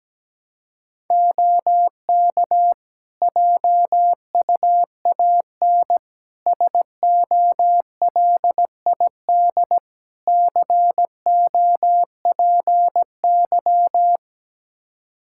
Morse code, heard as a keyed tone, about 17 wpm, 700 Hz. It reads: OK JUAN SOLID COPY
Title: Practice QSO - Exchange 3 - More Info - 17wpm